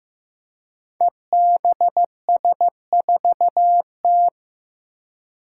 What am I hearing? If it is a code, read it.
Morse code, heard as a keyed tone, about 15 wpm, 700 Hz. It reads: EBS4T
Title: Five Challenging Letter-Number Combinations 15wpm